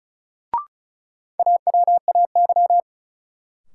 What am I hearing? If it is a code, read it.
Morse code, heard as a keyed tone, about 36 wpm, 700 Hz. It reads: AWAY
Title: Sentences from Top 600 Words - New Word Drill - 35wpm